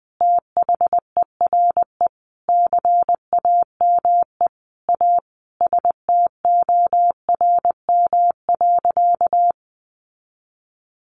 Morse code, heard as a keyed tone, about 20 wpm, 700 Hz. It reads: THERE CAME A STORM.
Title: Sentences from Top 1500 Words 20wpm